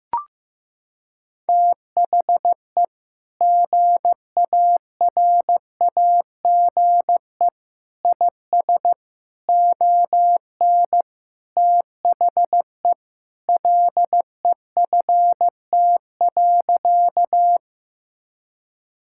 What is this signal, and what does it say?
Morse code, heard as a keyed tone, about 15 wpm, 700 Hz. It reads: THE GARAGE IS ON THE LEFT.
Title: Sentences from Top 1600 Words 15wpm